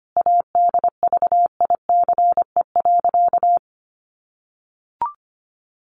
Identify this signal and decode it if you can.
Morse code, heard as a keyed tone, about 25 wpm, 700 Hz. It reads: ADVICE.
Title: Sentences from Top 1100 Words 25wpm